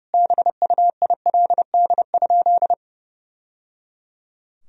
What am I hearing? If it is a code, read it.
Morse code, heard as a keyed tone, about 30 wpm, 700 Hz. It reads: BUILD?
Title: Sentences from Top 1000 Words - Review Part 3 - 30wpm